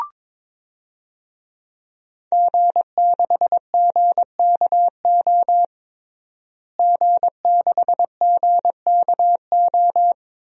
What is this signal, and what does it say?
Morse code, heard as a keyed tone, about 22 wpm, 700 Hz. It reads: G6GKO G6GKO
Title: DXCC-100 Call Signs - 5 Characters - Repeated 3x - 22wpm